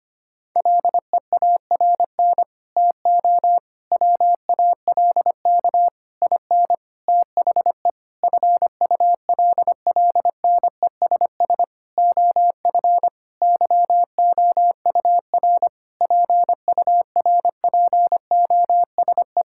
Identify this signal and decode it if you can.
Morse code, heard as a keyed tone, about 25 wpm, 700 Hz. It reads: LEARN TO WALK IN THE FULLNESS OF YOUR PURPOSE
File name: practice / 25wpm / Sentences with 9-Word Phrases - 25wpm